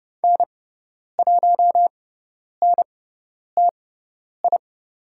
Morse code, heard as a keyed tone, about 30 wpm, spaced as for 10 wpm, 700 Hz. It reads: N1NTI
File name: Lesson 17g - Letter u - Call Signs 30wpm